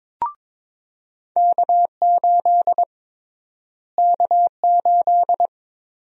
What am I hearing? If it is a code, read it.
Morse code, heard as a keyed tone, about 22 wpm, 700 Hz. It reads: K8 K8
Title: USA Call Sign Prefixes - Repeated 2x - 22wpm